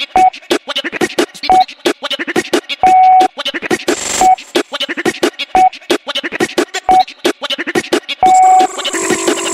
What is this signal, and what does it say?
There is no code.